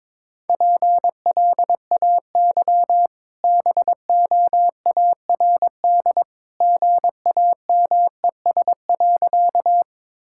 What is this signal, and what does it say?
Morse code, heard as a keyed tone, about 22 wpm, 690 Hz. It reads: PLAY BOARD GAMES.
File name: practice / 22wpm / Easy Sentences with Hints - 22wpm